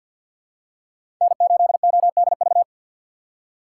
Morse code, heard as a keyed tone, about 52 wpm, 700 Hz. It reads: N8ODV
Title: USA Call Signs - 2x Character Spacing - 50wpm